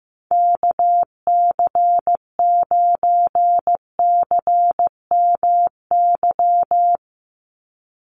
Morse code, heard as a keyed tone, about 15 wpm, 700 Hz. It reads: KC9CMY
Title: USA Call Signs 15wpm